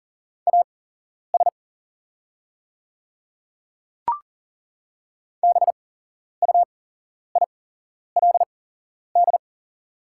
Morse code, heard as a keyed tone, about 40 wpm, spaced as for 11 wpm, 700 Hz. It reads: AS BUILD
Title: 8x Character Spacing - Top 300 Words - 40wpm